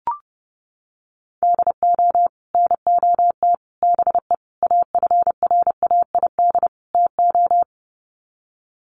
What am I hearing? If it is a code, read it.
Morse code, heard as a keyed tone, about 30 wpm, 700 Hz. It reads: DO NOT BE AFRAID TO
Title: Sets of 5 Words Encore2 30wpm